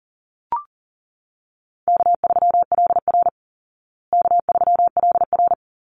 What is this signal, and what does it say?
Morse code, heard as a keyed tone, about 40 wpm, 700 Hz. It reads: K3LR K3LR
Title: Call Signs Repeated 2x - 40wpm